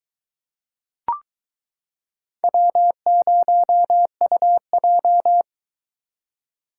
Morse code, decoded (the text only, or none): W0UJ